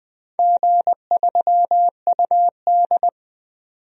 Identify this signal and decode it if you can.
Morse code, heard as a keyed tone, about 20 wpm, 700 Hz. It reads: G3UD
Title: DXCC-100 Call Signs - 4 Characters - 20wpm